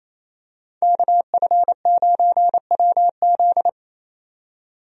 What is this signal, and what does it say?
Morse code, heard as a keyed tone, about 28 wpm, 700 Hz. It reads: KF9WZ
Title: USA Call Signs - 5 Characters - 28wpm